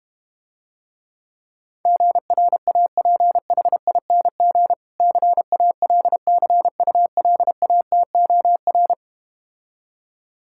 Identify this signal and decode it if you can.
Morse code, heard as a keyed tone, about 32 wpm, 700 Hz. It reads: GRAPHING CALCULATOR